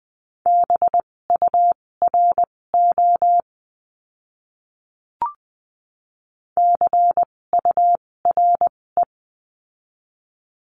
Morse code, heard as a keyed tone, about 20 wpm, spaced as for 16 wpm, 700 Hz. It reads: BURO CURE
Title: 256 Lesson 42a - Words 12-20wpm